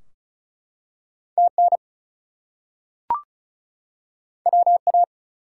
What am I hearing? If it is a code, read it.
Morse code, heard as a keyed tone, about 36 wpm, 700 Hz. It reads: TN WA
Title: US States and Canadian Provinces 35wpm